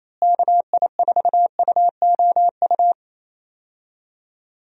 Morse code, heard as a keyed tone, about 28 wpm, 700 Hz. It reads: KI4UOU
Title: USA Call Signs - 6 Characters - Repeated 3x - 28wpm